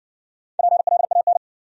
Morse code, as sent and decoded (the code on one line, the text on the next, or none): .--. .-.. .- -.
PLAN